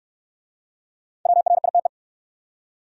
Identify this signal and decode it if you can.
Morse code, heard as a keyed tone, about 66 wpm, 700 Hz. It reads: WRITE